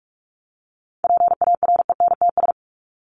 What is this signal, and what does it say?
Morse code, heard as a keyed tone, about 44 wpm, 700 Hz. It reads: PARENTS
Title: ES:S Suffix - Top 100 Words with Base Words 45wpm